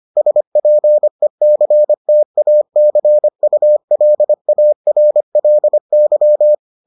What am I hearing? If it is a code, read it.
Morse code, heard as a keyed tone, about 25 wpm, 600 Hz. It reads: SPECTACULARLY